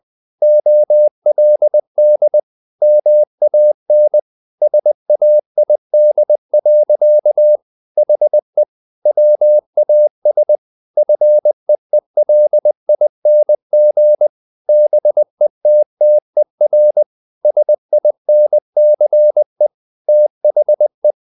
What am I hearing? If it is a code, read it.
Morse code, heard as a keyed tone, about 20 wpm, 600 Hz. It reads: OLD MAN SAID. HE WAS FEELING BETTER SINCE THE
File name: the old man and the sea part 2 20wpm